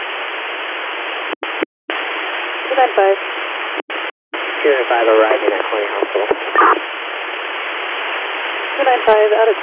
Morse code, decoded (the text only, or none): none